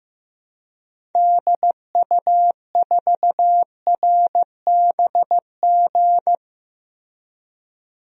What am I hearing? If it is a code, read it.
Morse code, heard as a keyed tone, about 15 wpm, 700 Hz. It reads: DU4RBG
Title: DXCC-100 Call Signs - 6 Characters - Repeated 3x - 15wpm